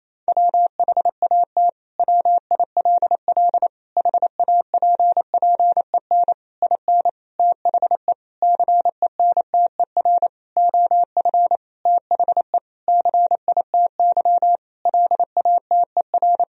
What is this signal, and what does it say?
Morse code, heard as a keyed tone, about 28 wpm, 700 Hz. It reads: WHAT WILL HAPPEN IN THE CENTER OF THE CITY LATER
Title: Sentences from Top 1000 Words - Review Part 2 - 28wpm